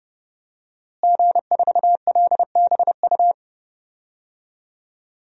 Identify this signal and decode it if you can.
Morse code, heard as a keyed tone, about 30 wpm, 700 Hz. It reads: G4LBU
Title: DXCC-100 Call Signs - 5 Characters - Repeated 3x - 30wpm